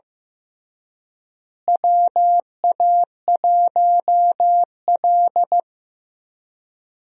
Morse code, heard as a keyed tone, about 15 wpm, 700 Hz. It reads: WA1L
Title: USA Call Signs - 4 Characters - Repeated 3x - 15wpm